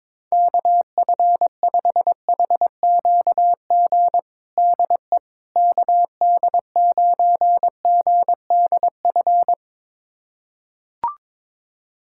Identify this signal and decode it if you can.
Morse code, heard as a keyed tone, about 22 wpm, 700 Hz. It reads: KF5HQG DE KD9GDF
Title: USA Callsign DE Callsign 22wpm